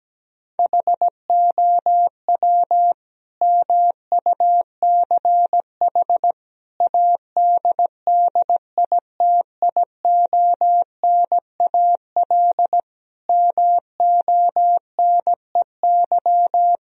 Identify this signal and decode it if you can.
Morse code, heard as a keyed tone, about 17 wpm, 700 Hz. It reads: HOW MUCH ADDITIONAL MONEY
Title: Sentences from Top 1000 Words - Review Part 4 - 17wpm